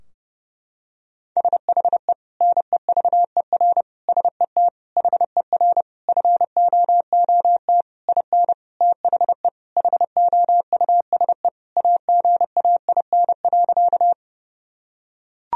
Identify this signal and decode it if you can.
Morse code, heard as a keyed tone, about 30 wpm, 700 Hz. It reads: SHE NEVER SET HER FOOT IN THE HOUSE AGAIN.
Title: Sentences from Top 1000 Words - Review Part 3 - 30wpm